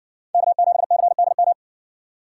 Morse code, heard as a keyed tone, about 57 wpm, 700 Hz. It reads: K7CDM